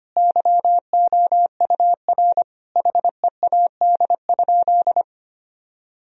Morse code, heard as a keyed tone, about 25 wpm, 700 Hz. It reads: YOUR HEAD?
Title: Sentences from Top 1000 Words - Review Part 3 - 25wpm